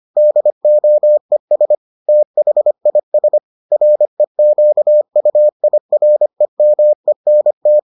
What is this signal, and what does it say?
Morse code, heard as a keyed tone, about 25 wpm, 600 Hz. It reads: DOES THIS REQUIREMENT